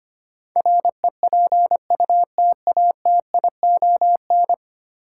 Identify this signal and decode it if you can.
Morse code, heard as a keyed tone, about 25 wpm, 700 Hz. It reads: REPUTATION
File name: Ten Letter Words - 25wpm